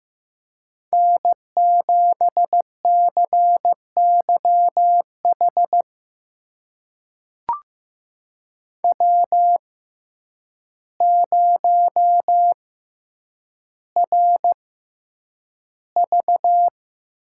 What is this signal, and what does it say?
Morse code, heard as a keyed tone, about 15 wpm, spaced as for 8 wpm, 700 Hz. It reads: N7CYH W0RV
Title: USA Call Signs - 5 Characters - 6x Character Spacing - 15wpm